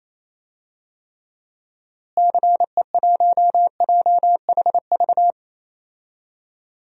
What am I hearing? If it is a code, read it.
Morse code, heard as a keyed tone, about 28 wpm, 700 Hz. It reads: CE1JHV
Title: DXCC-100 Call Signs - 6 Characters - Repeated 3x - 28wpm